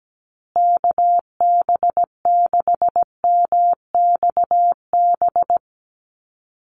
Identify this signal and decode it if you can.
Morse code, heard as a keyed tone, about 17 wpm, 700 Hz. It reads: KB6MXB
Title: USA Call Signs Repeated 2x 17wpm